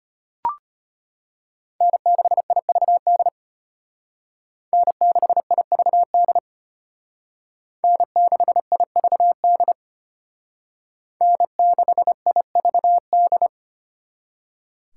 Morse code, heard as a keyed tone, about 32 wpm, 700 Hz. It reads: N6IVD N6IVD N6IVD N6IVD